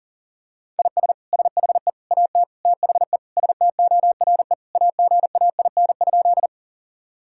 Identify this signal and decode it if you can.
Morse code, heard as a keyed tone, about 40 wpm, 700 Hz. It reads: IS SHE AT THE STORE AGAIN?
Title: Sentences from Top 1000 Words - Review Part 5 - 40wpm